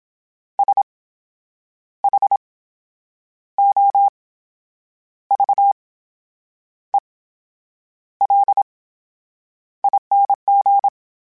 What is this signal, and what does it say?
Morse code, heard as a keyed tone, about 27 wpm, spaced as for 8 wpm, 800 Hz. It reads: SHOVELING
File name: Special INGs / Special INGs-mp3s / VerySlowToFaster Ing @25